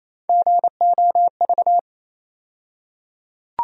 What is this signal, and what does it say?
Morse code, heard as a keyed tone, about 27 wpm, 700 Hz. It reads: GOV
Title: USA Call Sign Postfixes - 3 Letters 28wpm